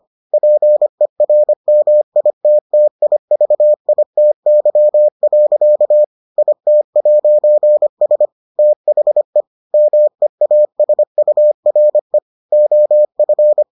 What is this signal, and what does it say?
Morse code, heard as a keyed tone, about 25 wpm, 600 Hz. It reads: PERMITTIVITY. IT'S THE MEASURE OF